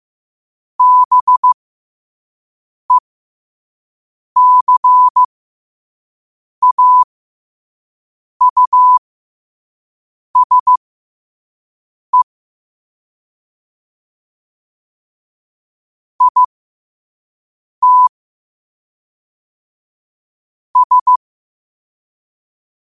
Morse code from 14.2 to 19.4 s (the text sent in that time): IT